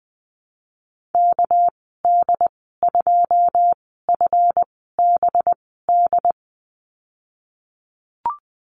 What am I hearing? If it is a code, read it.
Morse code, heard as a keyed tone, about 19 wpm, spaced as for 14 wpm, 700 Hz. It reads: KD2FBD
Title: 200 Lesson 32g - Letter k - Call Signs 20wpm